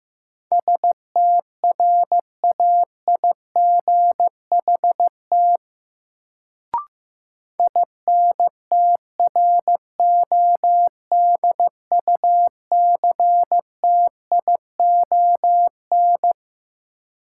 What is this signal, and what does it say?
Morse code, heard as a keyed tone, about 15 wpm, 700 Hz. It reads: STRAIGHT INTRODUCTION
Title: Sentences from Top 1000 Words - New Word Drill - 15wpm